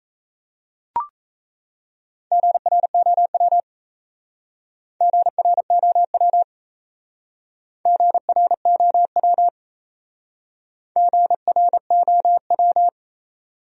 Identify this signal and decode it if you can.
Morse code, heard as a keyed tone, about 34 wpm, 700 Hz. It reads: GROW GROW GROW GROW
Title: Speed-Racing Top 300 Words 28wpm